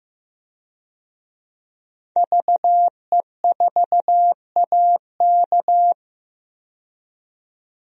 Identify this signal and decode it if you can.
Morse code, heard as a keyed tone, about 15 wpm, 700 Hz. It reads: VE4AK
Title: DXCC-100 Call Signs - 5 Characters - 15wpm